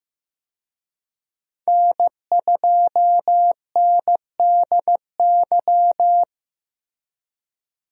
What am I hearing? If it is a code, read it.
Morse code, heard as a keyed tone, about 15 wpm, 700 Hz. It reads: N2NDY
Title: USA Call Signs - 5 Characters - Repeated 2x - 15wpm